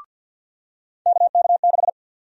Morse code, heard as a keyed tone, about 52 wpm, 700 Hz. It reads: KK6
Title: USA Call Sign Prefixes - 50wpm